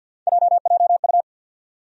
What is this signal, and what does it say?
Morse code, heard as a keyed tone, about 53 wpm, 700 Hz. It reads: JJU
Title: USA Call Sign Postfixes - 3 Letters 50wpm